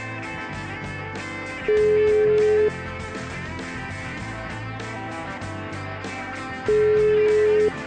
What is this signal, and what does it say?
Ringback tone (Europe, 425 Hz), with noise.